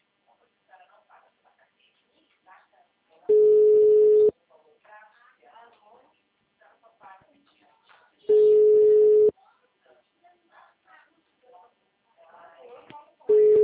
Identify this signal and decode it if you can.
Ringback tone (Europe, 425 Hz), with speech.
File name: out- -2025- - - .97645